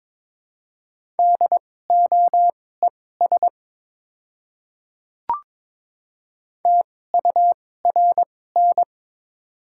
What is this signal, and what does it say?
Morse code, heard as a keyed tone, about 21 wpm, spaced as for 16 wpm, 700 Hz. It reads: DOES TURN
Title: 2x Character Spacing - Top 300 Words - 22wpm